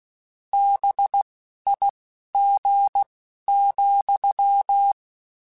Morse code, heard as a keyed tone, about 16 wpm, spaced as for 11 wpm, 800 Hz. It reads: BIG,